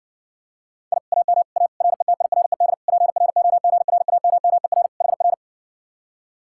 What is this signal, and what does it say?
Morse code, heard as a keyed tone, about 69 wpm, 700 Hz. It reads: I AM A RETIRED PROGRAMMER HR